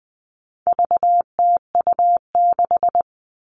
Morse code, heard as a keyed tone, about 20 wpm, 700 Hz. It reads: VTU6
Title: All Cut Numbers - 4-Digit 20wpm